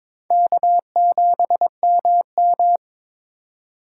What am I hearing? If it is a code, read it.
Morse code, heard as a keyed tone, about 21 wpm, 700 Hz. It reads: K7MM
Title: USA Call Signs - 4 Characters - 2x Character Spacing - 22wpm